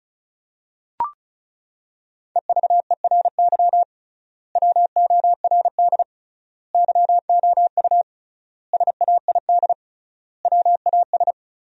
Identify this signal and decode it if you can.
Morse code, heard as a keyed tone, about 36 wpm, 700 Hz. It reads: EVERY WORD YOU SAID WAS
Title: 3x Word-Spacing - Sentences from Top 400 Words - 35wpm